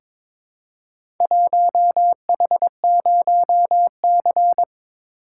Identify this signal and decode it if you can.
Morse code, heard as a keyed tone, about 23 wpm, 700 Hz. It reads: 1H0C
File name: practice / 22wpm / Four Letter-Number Combinations - 22wpm